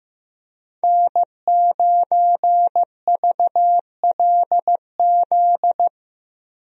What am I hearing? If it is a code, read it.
Morse code, heard as a keyed tone, about 15 wpm, 700 Hz. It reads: N9VLZ
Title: USA Call Signs - 5 Characters - Repeated 2x - 15wpm